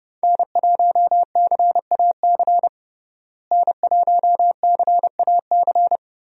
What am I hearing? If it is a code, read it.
Morse code, heard as a keyed tone, about 32 wpm, 700 Hz. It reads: N1CAC N1CAC
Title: USA Call Signs - 5 Characters - Repeated 2x - 30wpm